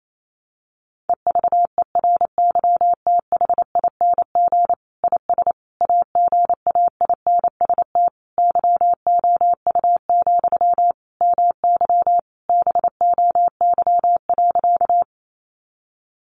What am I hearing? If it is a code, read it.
Morse code, heard as a keyed tone, about 28 wpm, 700 Hz. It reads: EVERYTHING IS AGAINST YOU, MY BOY.
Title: Sentences from Top 700 Words 28wpm